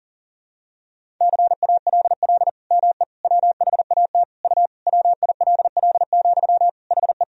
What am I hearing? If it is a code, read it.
Morse code, heard as a keyed tone, about 37 wpm, 700 Hz. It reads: CALL ME WHAT U WILL, HE
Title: Sentences from Top 200 Words and Abbreviations 40wpm